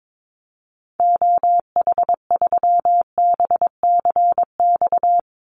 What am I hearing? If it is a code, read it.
Morse code, heard as a keyed tone, about 22 wpm, 700 Hz. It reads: OH3BCX
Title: Call Signs Repeated 2x - 22wpm